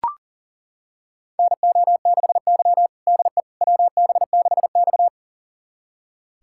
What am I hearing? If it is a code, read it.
Morse code, heard as a keyed tone, about 40 wpm, 700 Hz. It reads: NO6Y DE WB6X
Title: USA Callsign DE Callsign - 4 Characters - 40wpm